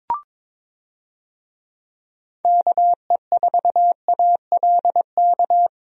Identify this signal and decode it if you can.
Morse code, heard as a keyed tone, about 22 wpm, 700 Hz. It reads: KE4ALK
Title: USA Call Signs - 6 Characters - Repeated 3x - 22wpm